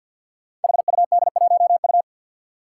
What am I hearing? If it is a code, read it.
Morse code, heard as a keyed tone, about 52 wpm, 700 Hz. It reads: HUD1U
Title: Five Challenging Letter-Number Combinations 50wpm